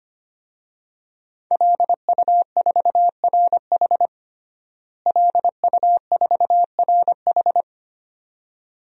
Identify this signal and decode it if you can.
Morse code, heard as a keyed tone, about 25 wpm, 700 Hz. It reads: LU4RH LU4RH
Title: DXCC-100 Call Signs - 5 Characters - Repeated 2x - 25wpm